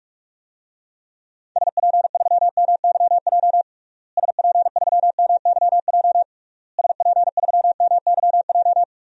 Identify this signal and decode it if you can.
Morse code, heard as a keyed tone, about 46 wpm, 700 Hz. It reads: SP3MYJ SP3MYJ SP3MYJ
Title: DXCC-100 Call Signs - 6 Characters - Repeated 3x - 45wpm